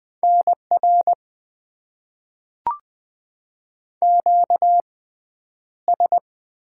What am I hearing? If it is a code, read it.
Morse code, heard as a keyed tone, about 20 wpm, spaced as for 10 wpm, 700 Hz. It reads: NR QS
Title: Top 90 QSO Elements - 6x Character Spacing - 20wpm